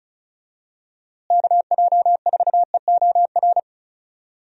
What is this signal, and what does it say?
Morse code, heard as a keyed tone, about 36 wpm, 700 Hz. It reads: KJ4EOR
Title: USA Call Signs - 6 Characters - Repeated 2x - 35wpm